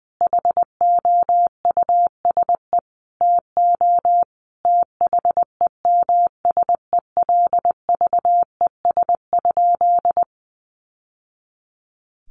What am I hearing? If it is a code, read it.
Morse code, heard as a keyed tone, about 20 wpm, 700 Hz. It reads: HOUSE TO THEMSELVES?
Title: Sentences from Top 800 Words 20wpm